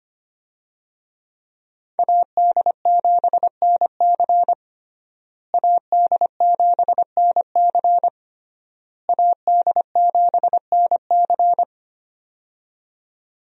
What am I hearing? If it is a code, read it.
Morse code, heard as a keyed tone, about 25 wpm, 700 Hz. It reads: AD7NC AD7NC AD7NC
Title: USA Call Signs - 5 Characters - Repeated 3x - 25wpm